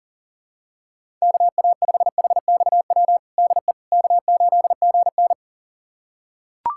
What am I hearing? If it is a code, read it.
Morse code, heard as a keyed tone, about 40 wpm, 700 Hz. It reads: KA5HXW DE K8GN
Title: USA Callsign DE Callsign 40wpm